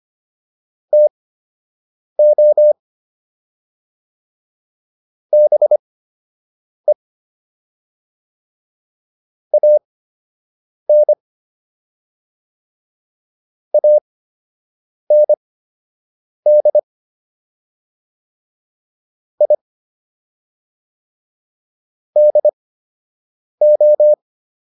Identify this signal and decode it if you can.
Morse code, heard as a keyed tone, about 25 wpm, spaced as for 7 wpm, 600 Hz. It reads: TO BE AN AND I DO